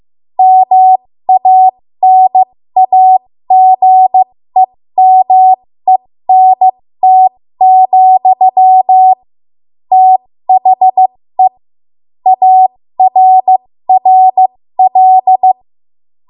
Morse code, heard as a keyed tone, about 15 wpm, spaced as for 13 wpm, 750 Hz. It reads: MANAGEMENT, THE ARRL
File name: Morse / 13WPM